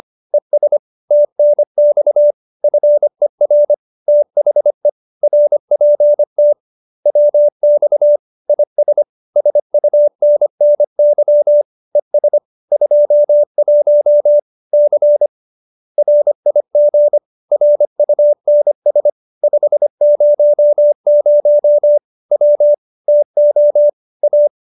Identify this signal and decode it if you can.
Morse code, heard as a keyed tone, about 25 wpm, 590 Hz. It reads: ES TNX FER THE RPT WX IS SUNNY ES 21 C RIG RUNS 500 W TO A